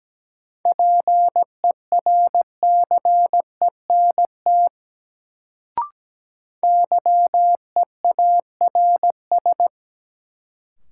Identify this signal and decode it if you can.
Morse code, heard as a keyed tone, about 17 wpm, 700 Hz. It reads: PERCENT YEARS
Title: Sets of 2 Words - New Word Drill - 17wpm